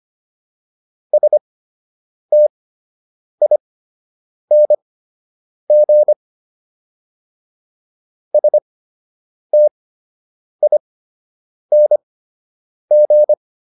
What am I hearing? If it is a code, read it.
Morse code, heard as a keyed tone, about 25 wpm, spaced as for 8 wpm, 600 Hz. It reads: STING STING